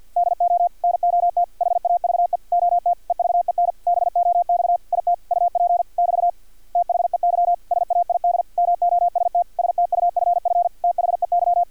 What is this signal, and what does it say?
Morse code, heard as a keyed tone, about 50 wpm, 700 Hz, with radio noise. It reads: DO NOT HAVE OT EVEN BOX IT UP <BT> THEY SAID MOST STUFF THEY
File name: archived / CRSnet